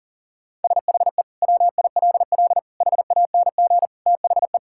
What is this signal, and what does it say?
Morse code, heard as a keyed tone, about 40 wpm, 700 Hz. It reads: SHE WILL HANG THE
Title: Sentences from Top 1600 Words 40wpm